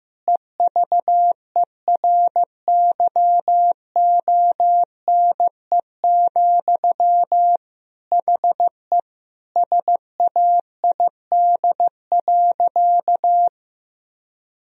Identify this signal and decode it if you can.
Morse code, heard as a keyed tone, about 15 wpm, 700 Hz. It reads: EVERYONE, HE SAID.
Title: Sentences from Top 1000 Words - Review Part 3 - 15wpm